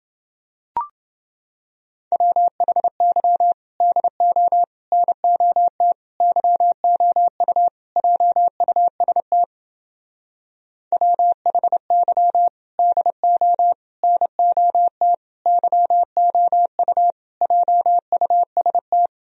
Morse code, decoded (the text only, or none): WHY DO NOT YOU JUST WHY DO NOT YOU JUST